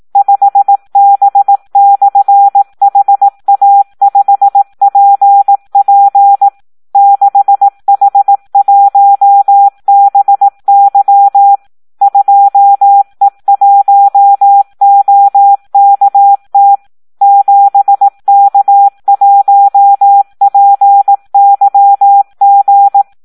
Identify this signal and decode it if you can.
Morse code, heard as a keyed tone, about 18 wpm, 800 Hz. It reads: 5B/HA5PP 6H1BY 2E1OKT 7K1PYG
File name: Audio practica CW / SD 1 18